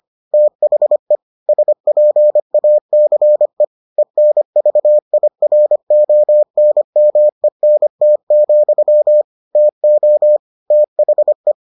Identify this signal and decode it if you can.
Morse code, heard as a keyed tone, about 25 wpm, 600 Hz. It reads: THE SPACE ENVIRONMENT, TO THE